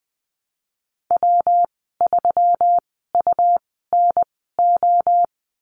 Morse code, heard as a keyed tone, about 20 wpm, spaced as for 15 wpm, 700 Hz. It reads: W3UNO
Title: 179 Lesson 29g - Slash - Call Signs 20wpm